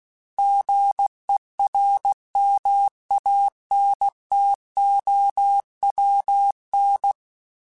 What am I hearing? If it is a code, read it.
Morse code, heard as a keyed tone, about 16 wpm, 800 Hz. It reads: GERMANTOWN
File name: fromSMs / FromSMs mp3s / Cities@15